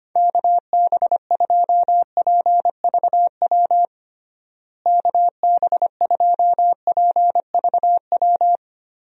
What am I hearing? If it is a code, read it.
Morse code, heard as a keyed tone, about 25 wpm, 700 Hz. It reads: KB2PVW KB2PVW
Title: USA Call Signs - 6 Characters - Repeated 2x - 25wpm